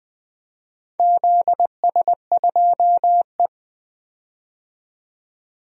Morse code, heard as a keyed tone, about 20 wpm, 700 Hz. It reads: ZS2E
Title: DXCC-100 Call Signs - 4 Characters - 20wpm